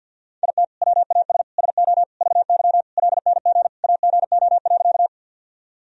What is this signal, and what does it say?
Morse code, heard as a keyed tone, about 51 wpm, 700 Hz. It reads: IT WAS SO VY LNG AGO.
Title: Sentences from Top 500 Words and Abbreviations 50wpm